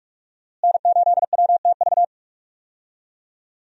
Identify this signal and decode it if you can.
Morse code, heard as a keyed tone, about 46 wpm, 700 Hz. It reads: N8WTV
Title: USA Call Signs - 5 Characters - Repeated 2x - 45wpm